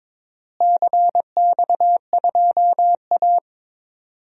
Morse code, heard as a keyed tone, about 22 wpm, 700 Hz. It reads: CX2A
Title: DXCC-100 Call Signs - 4 Characters - Repeated 3x - 22wpm